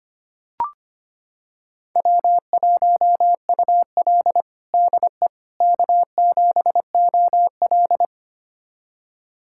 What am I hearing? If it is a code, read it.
Morse code, heard as a keyed tone, about 25 wpm, 700 Hz. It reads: W1UL DE K7OL
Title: USA Callsign DE Callsign - 4 Characters - 25wpm